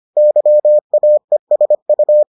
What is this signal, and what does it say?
Morse code, heard as a keyed tone, about 25 wpm, 600 Hz. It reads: YAESU